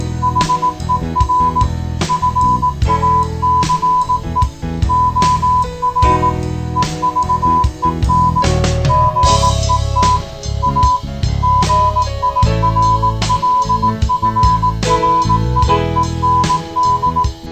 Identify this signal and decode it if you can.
Morse code, heard as a keyed tone, about 18 wpm, 1000 Hz. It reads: HER FACE OR EVEN GET A GIRLFRIEND